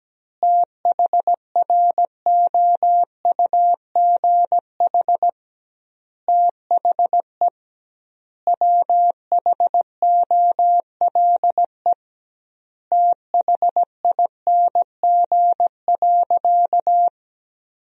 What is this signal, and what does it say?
Morse code, heard as a keyed tone, about 17 wpm, 700 Hz. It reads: THROUGH THE WHOLE THING.
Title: 2x Word-Spacing - Sentences from Top 400 Words - 17wpm